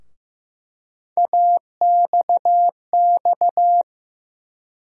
Morse code, heard as a keyed tone, about 15 wpm, 700 Hz. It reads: AXX
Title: USA Call Sign Postfixes - 3 Letters 15wpm